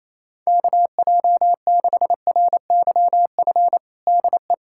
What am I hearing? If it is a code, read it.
Morse code, heard as a keyed tone, about 28 wpm, 700 Hz. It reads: KJ6RYF DE
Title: USA Callsign DE Callsign - 6 Characters - 28wpm